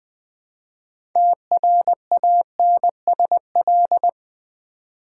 Morse code, heard as a keyed tone, about 20 wpm, 700 Hz. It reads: TRANSL
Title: Build A Word - 3 Letters At A Time - 9-Letter Words - 20wpm